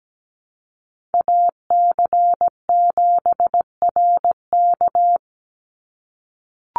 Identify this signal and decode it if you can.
Morse code, heard as a keyed tone, about 17 wpm, 700 Hz. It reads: AC7RK
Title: USA Call Signs 17wpm